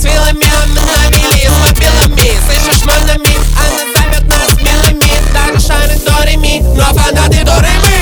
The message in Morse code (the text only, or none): none